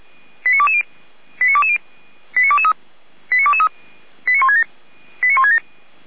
Five-tone selective call (ZVEI), heard as quick runs of five tones, tones of about 70 ms. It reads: 89220 89220 89223 89223 89167 89167